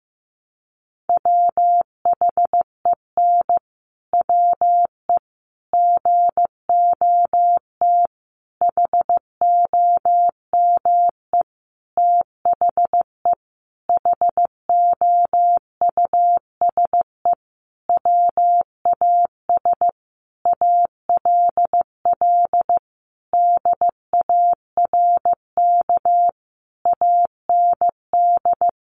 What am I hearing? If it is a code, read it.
Morse code, heard as a keyed tone, about 15 wpm, 700 Hz. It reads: WHEN WE GOT HOME THE HOUSE WAS ALL DARK AND
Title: Sentences from Top 500 Words 15wpm